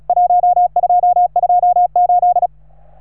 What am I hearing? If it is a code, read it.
Morse code, heard as a keyed tone, about 40 wpm, 710 Hz. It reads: 1228
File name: RS3A-20